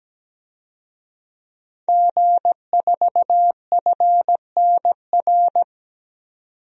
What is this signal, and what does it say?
Morse code, heard as a keyed tone, about 17 wpm, 700 Hz. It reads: G4FNR
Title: DXCC-100 Call Signs - 5 Characters - 17wpm